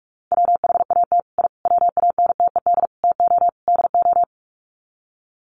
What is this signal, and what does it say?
Morse code, heard as a keyed tone, about 44 wpm, 700 Hz. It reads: WHAT I WANTED TO DO
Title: Sets of 5 Words Encore2 45wpm